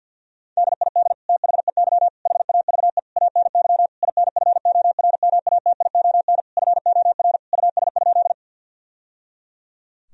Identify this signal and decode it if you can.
Morse code, heard as a keyed tone, about 50 wpm, 700 Hz. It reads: DID THEY HAVE ANY INFORMATION FOR US?
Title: Sentences from Top 1000 Words - Review Part 3 - 50wpm